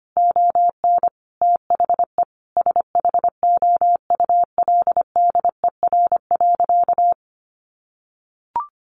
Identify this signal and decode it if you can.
Morse code, heard as a keyed tone, about 25 wpm, 700 Hz. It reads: ON THE SHOULDER.
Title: Additional Easy Phrases 25wpm